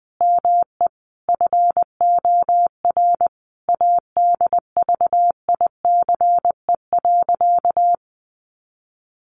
Morse code, heard as a keyed tone, about 20 wpm, 700 Hz. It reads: ME FOR ADVICE.
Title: Sentences from Top 1100 Words 20wpm